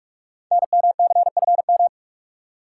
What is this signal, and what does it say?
Morse code, heard as a keyed tone, about 44 wpm, 700 Hz. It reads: NMCFM
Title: Code Groups - 5 Letters - 45wpm